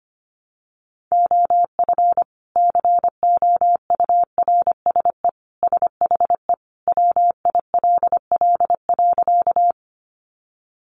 Morse code, heard as a keyed tone, about 25 wpm, 700 Hz. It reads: OF COURSE SHE WILL.
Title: Sentences from Top 500 Words 25wpm